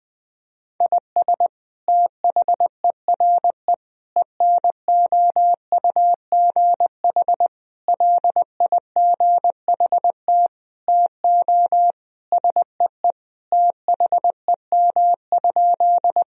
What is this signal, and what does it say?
Morse code, heard as a keyed tone, about 21 wpm, 700 Hz. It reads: IS THERE ENOUGH LIGHT TO SEE THEM?
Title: Sentences from Top 1000 Words - Review Part 1 - 20wpm